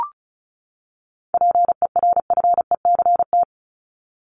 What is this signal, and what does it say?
Morse code, heard as a keyed tone, about 34 wpm, 700 Hz. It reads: PERFECT
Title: LY Suffix - Top 100 Words with Base Words 35wpm